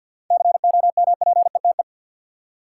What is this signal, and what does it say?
Morse code, heard as a keyed tone, about 50 wpm, 700 Hz. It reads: COMPETE
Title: Seven Letter Words - 50wpm